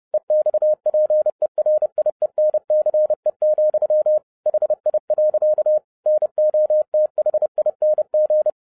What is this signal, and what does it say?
Morse code, heard as a keyed tone, about 30 wpm, 610 Hz. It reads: EXPERIENCE, HI. NOTHING